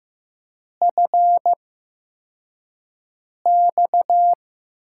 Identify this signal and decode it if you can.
Morse code, heard as a keyed tone, about 15 wpm, spaced as for 4 wpm, 700 Hz. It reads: FX
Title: USA Call Signs - 8x Character Spacing - 15wpm